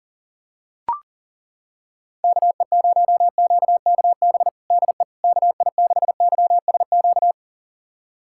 Morse code, heard as a keyed tone, about 40 wpm, 700 Hz. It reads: KE0QKB DE KI6YSQ
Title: USA Callsign DE Callsign - 6 Characters - 40wpm